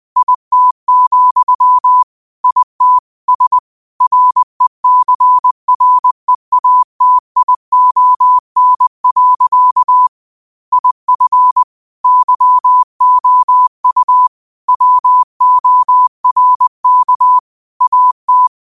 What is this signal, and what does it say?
Morse code, heard as a keyed tone, about 20 wpm, 1000 Hz. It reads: IT, IT S RECREATION. IF YOU WORK AT
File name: qotdfiles / 20wpm